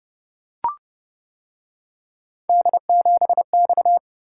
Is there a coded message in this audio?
Yes: Morse code from 2.5 to 4.0 s, heard as a keyed tone, about 29 wpm, 700 Hz. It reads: D7X